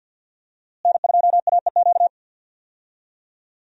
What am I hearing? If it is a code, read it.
Morse code, heard as a keyed tone, about 48 wpm, 700 Hz. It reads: N2AEQ
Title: USA Call Signs - 5 Characters - Repeated 3x - 50wpm